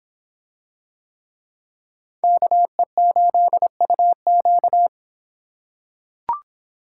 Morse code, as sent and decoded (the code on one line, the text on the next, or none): -.- . ---.. ..- --.-
KE8UQ